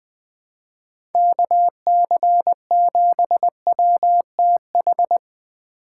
Morse code, heard as a keyed tone, about 20 wpm, 700 Hz. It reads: KC7WTH
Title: USA Call Signs - 6 Characters - 4x Character Spacing - 20wpm